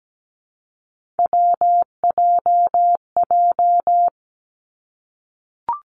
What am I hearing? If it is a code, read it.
Morse code, heard as a keyed tone, about 17 wpm, 700 Hz. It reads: WJJ
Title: Three Challenging Letter-Number Combinations 17wpm